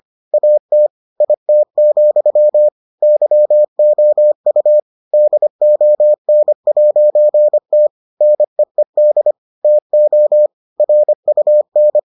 Morse code, heard as a keyed tone, about 25 wpm, 600 Hz. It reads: AT IT, YOU DON'T NEED TO RUN